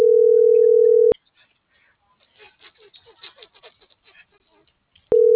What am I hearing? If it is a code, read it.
Ringback tone (North America), 440 + 480 Hz.